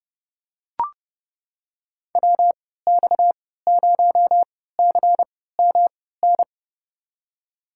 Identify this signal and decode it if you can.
Morse code, heard as a keyed tone, about 30 wpm, spaced as for 17 wpm, 700 Hz. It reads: WX0CMN
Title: USA Call Signs - 6 Characters - 3x Character Spacing - 30wpm